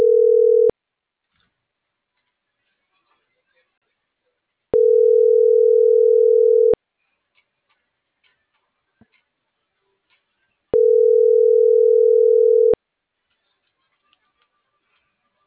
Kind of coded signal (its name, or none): telephone call progress tone